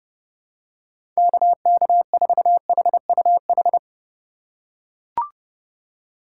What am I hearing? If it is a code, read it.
Morse code, heard as a keyed tone, about 31 wpm, 700 Hz. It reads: KK4HUH
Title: USA Call Signs - 2x Character Spacing - 30wpm